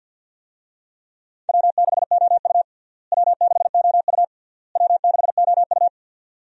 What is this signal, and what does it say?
Morse code, heard as a keyed tone, about 51 wpm, 700 Hz. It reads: W6OU W6OU W6OU
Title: USA Call Signs - 4 Characters - Repeated 3x - 50wpm